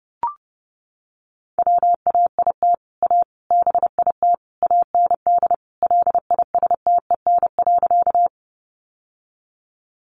Morse code, heard as a keyed tone, about 30 wpm, 700 Hz. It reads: WAIT A BIT AND LISTEN.
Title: Sentences from Top 900 Words 30wpm